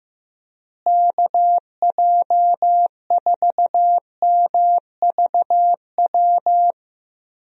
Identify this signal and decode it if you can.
Morse code, heard as a keyed tone, about 15 wpm, 700 Hz. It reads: KJ4MVW
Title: USA Call Signs - 6 Characters - Repeated 3x - 15wpm